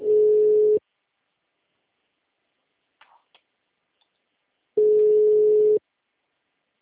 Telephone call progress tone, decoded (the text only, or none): ringback tone